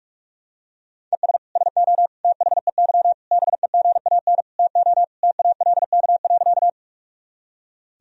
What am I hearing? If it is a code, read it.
Morse code, heard as a keyed tone, about 41 wpm, 700 Hz. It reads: ES SO THEY BEGAN TO TALK.
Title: Sentences from Top 400 Words and Abbreviations 45wpm